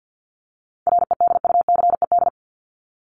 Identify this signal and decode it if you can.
Morse code, heard as a keyed tone, about 50 wpm, 700 Hz. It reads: REDUCED
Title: ED Suffix - Top 100 Words with Base Words 50wpm